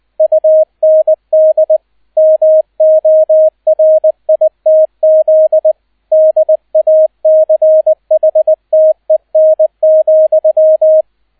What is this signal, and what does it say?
Morse code, heard as a keyed tone, about 20 wpm, 630 Hz, with radio noise. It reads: UND MORITZ DACHTEN,